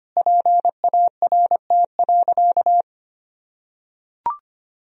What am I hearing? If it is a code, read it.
Morse code, heard as a keyed tone, about 25 wpm, 700 Hz. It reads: PART.
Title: Sentences from Top 1000 Words - Review Part 1 - 25wpm